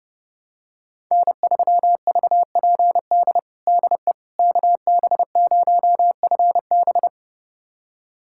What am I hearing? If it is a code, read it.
Morse code, heard as a keyed tone, about 30 wpm, 700 Hz. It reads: N3VPD DE KB0FB
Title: USA Callsign DE Callsign - 5 Characters - 30wpm